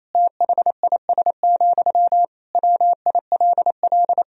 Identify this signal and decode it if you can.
Morse code, heard as a keyed tone, about 27 wpm, 700 Hz. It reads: THIS, WILL
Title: Sentences from Top 400 Words and Abbreviations 28wpm